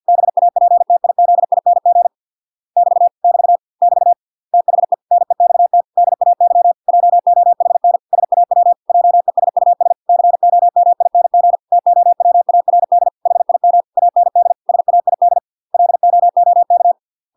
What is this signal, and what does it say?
Morse code, heard as a keyed tone, about 50 wpm, 700 Hz. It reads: BAPTIZING <BT> <BT> <BT> THE NEXT DAY JOHN SAW JESUS COMING TOWARD HIM AND SAID LOOK